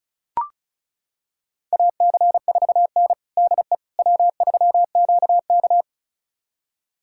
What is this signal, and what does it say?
Morse code, heard as a keyed tone, about 34 wpm, 700 Hz. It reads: AC4N DE W3QK
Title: USA Callsign DE Callsign - 4 Characters - 35wpm